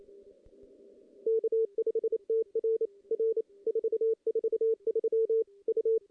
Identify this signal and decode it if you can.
Morse code, heard as a keyed tone, about 28 wpm, 410 Hz. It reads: K5TR R 443 U
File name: contests / CW-SS-k5tr- -1600